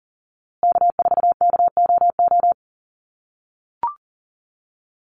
Morse code, heard as a keyed tone, about 40 wpm, 700 Hz. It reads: K4KOO
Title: USA Call Signs Repeated 2x 40wpm